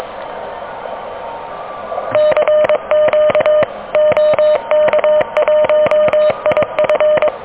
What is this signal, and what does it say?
Morse code, heard as a keyed tone, about 22 wpm, 600 Hz. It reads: CQ OK1IF
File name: OK QRP 2023 / OK QRP 2023